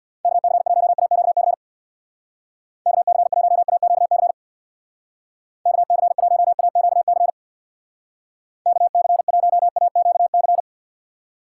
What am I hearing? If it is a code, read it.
Morse code, heard as a keyed tone, about 63 wpm, 700 Hz. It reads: KC1AQC KC1AQC KC1AQC KC1AQC